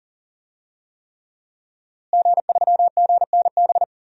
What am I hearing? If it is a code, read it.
Morse code, heard as a keyed tone, about 40 wpm, 700 Hz. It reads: G3GNB
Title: DXCC-100 Call Signs - 5 Characters - Repeated 3x - 40wpm